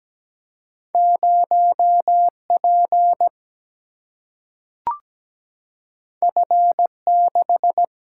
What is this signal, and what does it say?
Morse code, heard as a keyed tone, about 17 wpm, 700 Hz. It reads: 0 P F 6
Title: Single-Letter Single-Number 17wpm